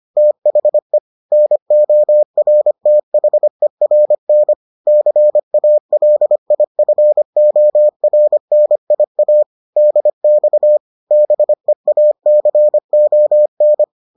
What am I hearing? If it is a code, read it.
Morse code, heard as a keyed tone, about 25 wpm, 600 Hz. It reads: THE NORTHERN CALIFORNIA DX BEACON